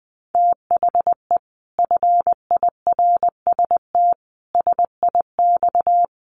Morse code, heard as a keyed tone, about 20 wpm, 700 Hz. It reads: THE FIRST SIX
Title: Sets of 5 Words Encore2 20wpm